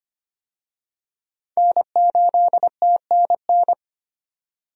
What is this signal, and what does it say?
Morse code, heard as a keyed tone, about 25 wpm, 700 Hz. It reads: N8TNN
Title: USA Call Signs - 5 Characters - 25wpm